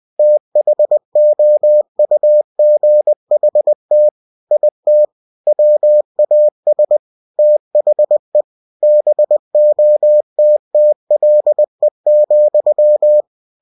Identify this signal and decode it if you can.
Morse code, heard as a keyed tone, about 20 wpm, 600 Hz. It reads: THOUGHT IT WAS THE BOTTLE,
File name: cw-books / winnie-the-pooh / winnie-20wpm-page-0092